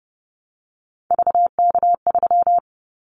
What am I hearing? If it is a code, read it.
Morse code, heard as a keyed tone, about 30 wpm, 700 Hz. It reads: VK3
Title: Callsign Prefixes for Top 100 DXCC - 30wpm